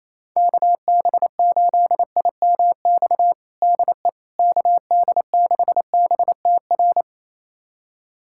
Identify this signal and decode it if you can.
Morse code, heard as a keyed tone, about 27 wpm, 700 Hz. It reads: KB8IMX DE KD6BTR
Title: USA Callsign DE Callsign 28wpm